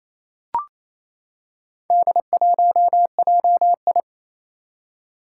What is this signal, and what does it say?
Morse code, heard as a keyed tone, about 29 wpm, 700 Hz. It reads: D1JI
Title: Four Challenging Letter-Number Combinations 28wpm